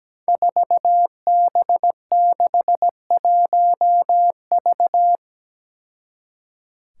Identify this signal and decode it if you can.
Morse code, heard as a keyed tone, about 17 wpm, 700 Hz. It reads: 4B61V
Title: Five Challenging Letter-Number Combinations 17wpm